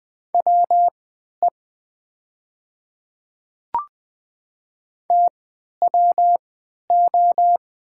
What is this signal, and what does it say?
Morse code, heard as a keyed tone, about 20 wpm, spaced as for 11 wpm, 700 Hz. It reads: WE TWO
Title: 3x Character Spacing - Top 200 Words - 20wpm